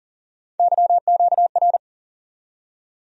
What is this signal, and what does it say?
Morse code, heard as a keyed tone, about 40 wpm, 700 Hz. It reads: YQR
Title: USA Call Sign Postfixes - 3 Letters 40wpm